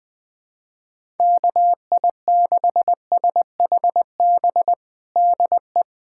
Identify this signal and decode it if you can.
Morse code, heard as a keyed tone, about 21 wpm, 700 Hz. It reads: KI6SHB DE
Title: USA Callsign DE Callsign 20wpm